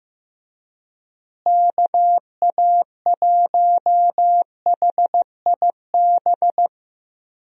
Morse code, heard as a keyed tone, about 16 wpm, 700 Hz. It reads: KA1HIB
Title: USA Call Signs - 6 Characters - Repeated 2x - 15wpm